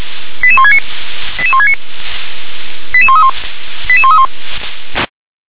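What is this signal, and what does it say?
Five-tone selective call (ZVEI), heard as quick runs of five tones, tones of about 70 ms. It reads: 88169 88169 88131 88131
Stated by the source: ZVEI five-tone call received by SDR.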